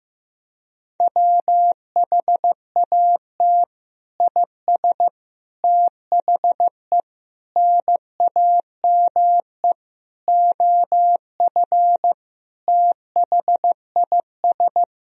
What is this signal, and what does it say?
Morse code, heard as a keyed tone, about 15 wpm, 700 Hz. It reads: WHAT IS THE NAME OF THIS
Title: Sentences from Top 1000 Words - Review Part 1 - 15wpm